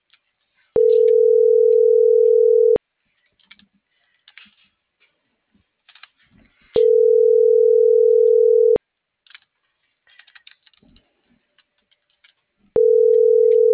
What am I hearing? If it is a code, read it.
Ringback tone (North America), 440 + 480 Hz.